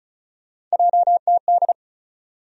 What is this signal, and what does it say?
Morse code, heard as a keyed tone, about 36 wpm, 700 Hz. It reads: JTD
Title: USA Call Sign Postfixes - 3 Letters 35wpm